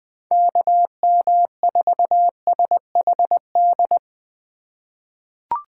Morse code, heard as a keyed tone, about 20 wpm, 700 Hz. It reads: KM4SHD
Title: USA Call Signs - 6 Characters - 2x Character Spacing - 20wpm